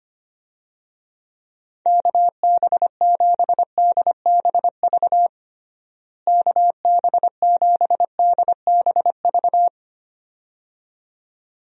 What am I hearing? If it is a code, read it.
Morse code, heard as a keyed tone, about 25 wpm, 700 Hz. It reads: KB7DBV KB7DBV
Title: USA Call Signs - 6 Characters - Repeated 2x - 25wpm